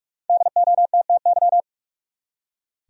Morse code, heard as a keyed tone, about 46 wpm, 700 Hz. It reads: DOTTY
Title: USA Common Names - Max 5 Letters - 45wpm